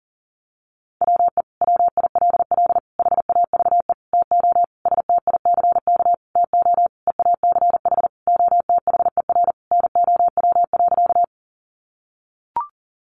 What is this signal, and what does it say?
Morse code, heard as a keyed tone, about 40 wpm, 700 Hz. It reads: WE WILL HAVE TO STICK TO EACH OTHER NOW.
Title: Sentences from Top 1200 Words 40wpm